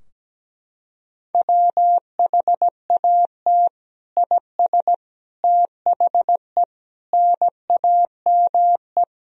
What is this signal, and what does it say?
Morse code, heard as a keyed tone, about 17 wpm, 700 Hz. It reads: WHAT IS THE NAME
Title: Sentences from Top 1000 Words - Review Part 3 - 17wpm